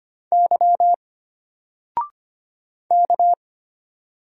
Morse code, heard as a keyed tone, about 25 wpm, 700 Hz. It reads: Y K
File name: Single-Letter-Number-25wpm